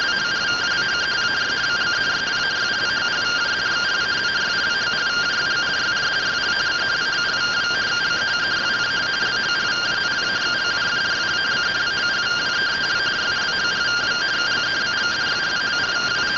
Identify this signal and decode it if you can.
Two-tone teleprinter signal, about 170 Hz shift.